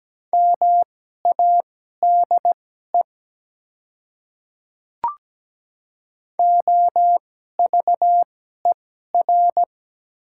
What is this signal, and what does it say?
Morse code, heard as a keyed tone, about 17 wpm, spaced as for 12 wpm, 700 Hz. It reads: MADE OVER
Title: 2x Character Spacing - Top 200 Words - 17wpm